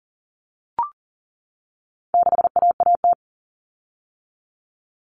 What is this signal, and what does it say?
Morse code, heard as a keyed tone, about 40 wpm, 700 Hz. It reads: 6AAT
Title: Common Cut Numbers - 4-Digit 40wpm